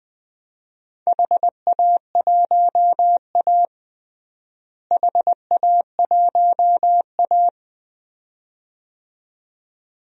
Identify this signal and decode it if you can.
Morse code, heard as a keyed tone, about 20 wpm, 700 Hz. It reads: HA1A HA1A
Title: DXCC-100 Call Signs - 4 Characters - Repeated 2x - 20wpm